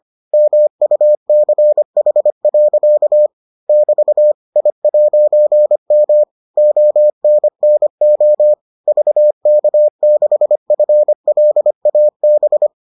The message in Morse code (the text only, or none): MUCH. <BT> I'M ONNO VK6FLAB